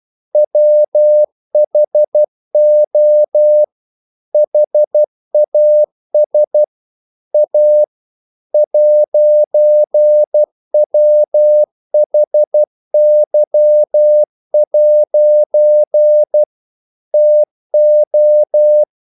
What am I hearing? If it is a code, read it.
Morse code, heard as a keyed tone, about 12 wpm, 600 Hz. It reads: WHO HAS A 'WHY' TO
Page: English quotations in CW (telegraphy - morse)